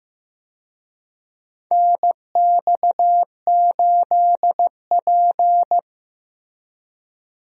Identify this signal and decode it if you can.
Morse code, heard as a keyed tone, about 15 wpm, 700 Hz. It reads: NX8P
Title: USA Call Signs - 4 Characters - Repeated 2x - 15wpm